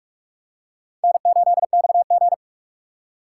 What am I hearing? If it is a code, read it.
Morse code, heard as a keyed tone, about 46 wpm, 700 Hz. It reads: N8XG
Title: USA Call Signs - 4 Characters - 2x Character Spacing - 45wpm